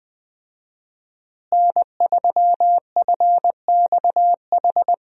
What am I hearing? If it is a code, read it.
Morse code, heard as a keyed tone, about 20 wpm, 700 Hz. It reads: N3FXH
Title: USA Call Signs - 5 Characters - Repeated 2x - 20wpm